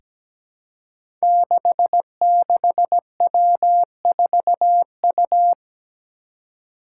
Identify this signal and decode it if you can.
Morse code, heard as a keyed tone, about 17 wpm, 700 Hz. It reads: 66W4U
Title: Five Challenging Letter-Number Combinations 17wpm